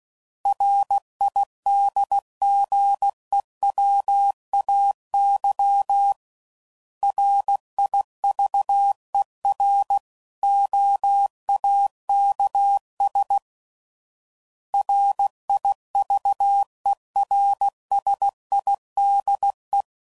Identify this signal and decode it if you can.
Morse code, heard as a keyed tone, about 16 wpm, 800 Hz. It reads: RIDGEWAY RIVEROAKS RIVERSIDE
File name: CW Code Groups @15WPM / fromSM's / Cities